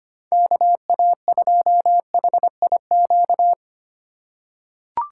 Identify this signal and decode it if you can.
Morse code, heard as a keyed tone, about 25 wpm, 700 Hz. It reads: KA2HIQ
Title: USA Call Signs - 6 Characters - 2x Character Spacing - 25wpm